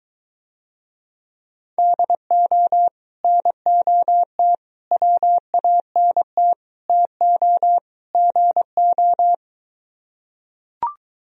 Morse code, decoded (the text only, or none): DO NOT WANT TO GO